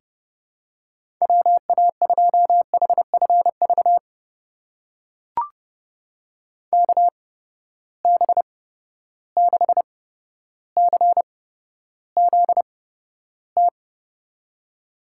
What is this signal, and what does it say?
Morse code, heard as a keyed tone, about 30 wpm, spaced as for 13 wpm, 700 Hz. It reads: WA2HFV KB6CZT
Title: USA Call Signs - 6 Characters - 8x Character Spacing - 30wpm